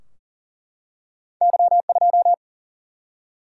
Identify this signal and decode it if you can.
Morse code, heard as a keyed tone, about 39 wpm, 700 Hz. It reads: Y2
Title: Single-Letter Single-Number 40wpm